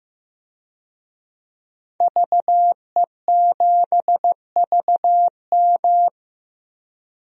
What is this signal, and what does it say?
Morse code, heard as a keyed tone, about 15 wpm, 700 Hz. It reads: VE7VM
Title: DXCC-100 Call Signs - 5 Characters - 15wpm